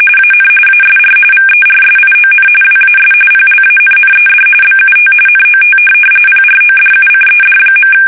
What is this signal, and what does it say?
RTTY, about 45.45 baud. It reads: AN ASR VERSION WAS PLANNED BUT NOT PRODUCED.
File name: fsk850 / teletype-corp